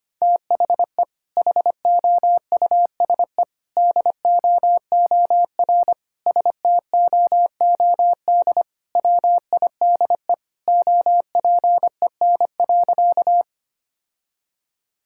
Morse code, heard as a keyed tone, about 25 wpm, 700 Hz. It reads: THE HOUSE DOOR STOOD WIDE OPEN.
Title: Sentences from Top 1000 Words - Review Part 6 - 25wpm